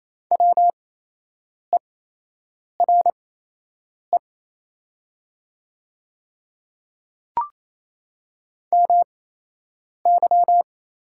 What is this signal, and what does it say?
Morse code, heard as a keyed tone, about 28 wpm, spaced as for 8 wpm, 700 Hz. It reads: WERE MY
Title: 8x Character Spacing - Top 100 Words - 28wpm